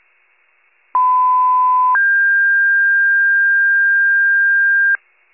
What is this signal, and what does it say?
Two-tone page, heard as a one-second tone, then a three-second tone, about 1010 then 1600 Hz. It is heard from 0.9 to 5.0 s.